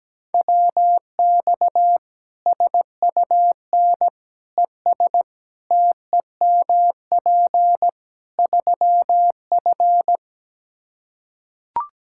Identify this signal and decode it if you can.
Morse code, heard as a keyed tone, about 17 wpm, 700 Hz. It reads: WX SUN ES TEMP 3F
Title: Practice QSO - Exchange 2 - Info - 17wpm